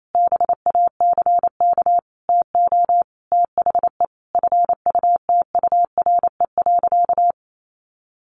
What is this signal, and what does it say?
Morse code, heard as a keyed tone, about 28 wpm, 700 Hz. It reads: BACK TO THE FUTURE.
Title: A Sound Of Thunder 28wpm